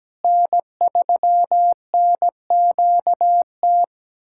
Morse code, heard as a keyed tone, about 17 wpm, 700 Hz. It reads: N3NQT
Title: USA Callsign DE Callsign - 5 Characters - 17wpm